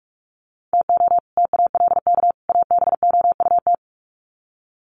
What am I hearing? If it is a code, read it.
Morse code, heard as a keyed tone, about 44 wpm, 700 Hz. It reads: TO TALK ABOUT
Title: Sets of 3 Words 45wpm